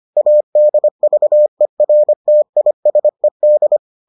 Morse code, heard as a keyed tone, about 25 wpm, 600 Hz. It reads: ADVERTISED